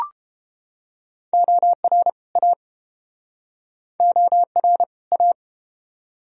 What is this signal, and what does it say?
Morse code, heard as a keyed tone, about 33 wpm, 700 Hz. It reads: OR A OR A